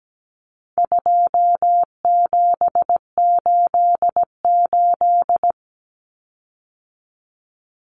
Morse code, heard as a keyed tone, about 17 wpm, 700 Hz. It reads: 2788
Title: Common Cut Numbers - 4-Digit 17wpm